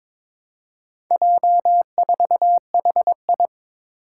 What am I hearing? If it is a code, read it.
Morse code, heard as a keyed tone, about 22 wpm, 700 Hz. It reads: J4HI
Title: Four Challenging Letter-Number Combinations 22wpm